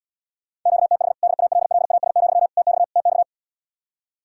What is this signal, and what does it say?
Morse code, heard as a keyed tone, about 77 wpm, 700 Hz. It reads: QTH DALLAS, TX TX